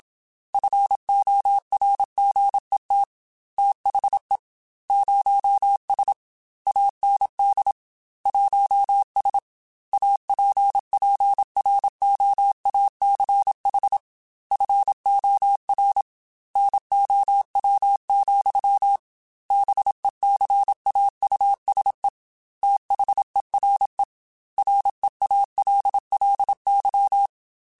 Morse code, heard as a keyed tone, about 27 wpm, 800 Hz. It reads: FORGET THE 0S AND 1S APPROACH FOR NOW, BECAUSE THERE REALLY
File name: Technical Subjects / Technical Subjects-mp3s / Gates Intro @25